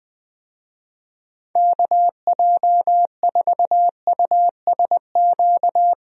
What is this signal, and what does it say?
Morse code, heard as a keyed tone, about 20 wpm, 700 Hz. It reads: KJ4USQ
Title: USA Call Signs - 6 Characters - Repeated 2x - 20wpm